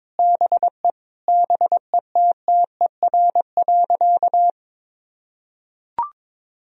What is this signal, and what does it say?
Morse code, heard as a keyed tone, about 22 wpm, 700 Hz. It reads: BE BETTER.
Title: Sentences from Top 1000 Words - Review Part 1 - 22wpm